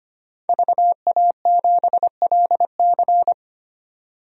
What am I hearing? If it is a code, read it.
Morse code, heard as a keyed tone, about 25 wpm, 700 Hz. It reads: VA7LC
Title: DXCC-100 Call Signs - 5 Characters - 25wpm